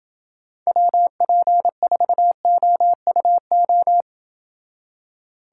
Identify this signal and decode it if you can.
Morse code, heard as a keyed tone, about 28 wpm, 700 Hz. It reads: WP4OUO